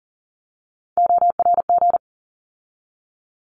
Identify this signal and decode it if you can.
Morse code, heard as a keyed tone, about 40 wpm, 700 Hz. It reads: ORG
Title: ARRL Sections 40wpm